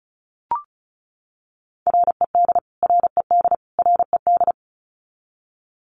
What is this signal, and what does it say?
Morse code, heard as a keyed tone, about 34 wpm, 700 Hz. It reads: RED RED RED
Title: Top 500 Words Repeated 35wpm